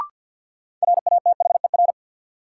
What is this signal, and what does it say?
Morse code, heard as a keyed tone, about 54 wpm, 700 Hz. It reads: RATHER
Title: Sentences from Top 700 Words - New Word Drill - 50wpm